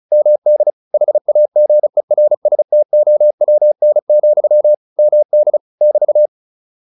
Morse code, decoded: MD HAGERSTOWN, MD <BT>

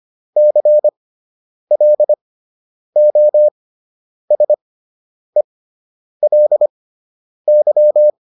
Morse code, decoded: CLOSELY

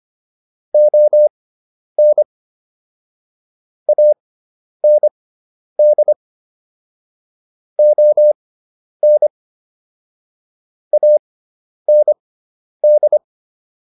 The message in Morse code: ON AND ON AND

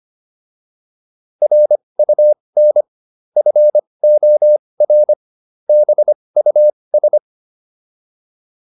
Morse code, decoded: RUN FOR BUS

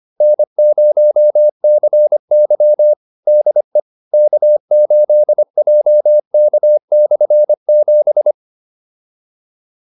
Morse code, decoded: N0CY DE K8JK/7

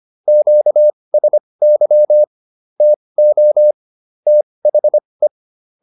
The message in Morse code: QSY TO THE

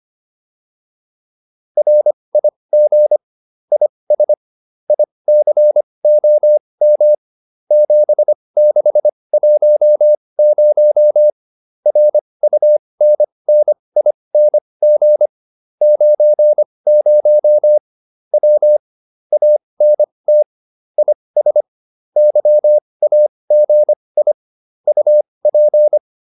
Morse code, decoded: RIG IS ICOM 7610 RUNNING 90 W ANT IS YAGI UP

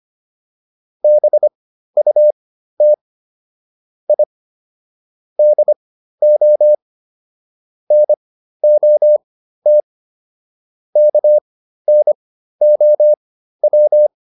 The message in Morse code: BUT I DO NOT KNOW

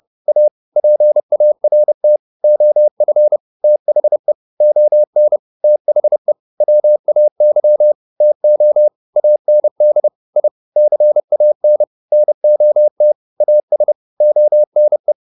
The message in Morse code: A PART OF THE ON THE WAY TO AND I CAN NOT AS ONE

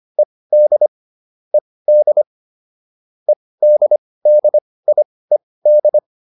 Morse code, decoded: ED ED EDDIED

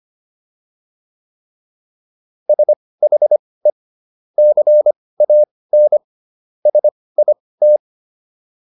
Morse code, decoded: SHE CAN SIT

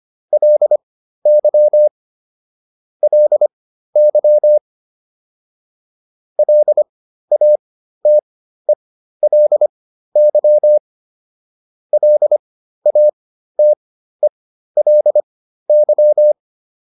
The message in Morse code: LY LY LATELY LATELY

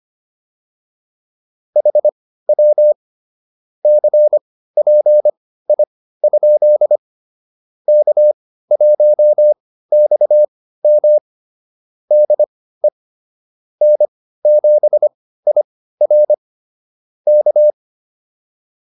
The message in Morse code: HW CPI? K1XM DE N7IR K

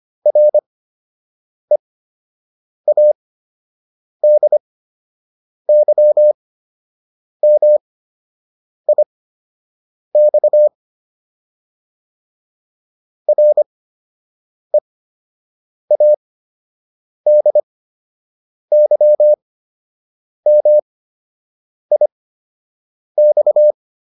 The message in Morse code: READYMIX READYMIX

